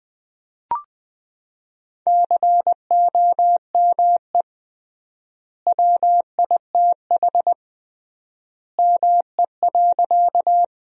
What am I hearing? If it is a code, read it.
Morse code, heard as a keyed tone, about 20 wpm, 700 Hz. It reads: COME WITH ME.
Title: 3x Word-Spacing - Sentences from Top 200 Words - 20wpm